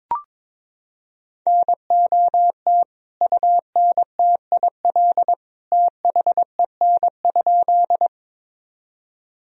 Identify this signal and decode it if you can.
Morse code, heard as a keyed tone, about 21 wpm, 700 Hz. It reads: NOT UNTIL THEN?
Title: Sentences from Top 1000 Words - Review Part 1 - 22wpm